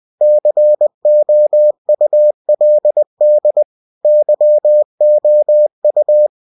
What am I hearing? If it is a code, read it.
Morse code, heard as a keyed tone, about 20 wpm, 600 Hz. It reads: COULD YOU
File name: cw-books / winnie-the-pooh / winnie-20wpm-page-0098